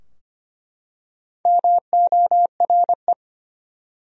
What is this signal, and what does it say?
Morse code, heard as a keyed tone, about 25 wpm, 700 Hz. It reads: MORE
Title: Top 100 Words 25wpm